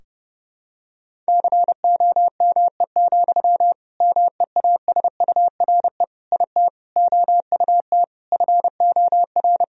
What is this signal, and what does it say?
Morse code, heard as a keyed tone, about 30 wpm, 700 Hz. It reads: COME, MEASURE IT OUT FOR
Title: Sentences from Top 1000 Words - Review Part 2 - 30wpm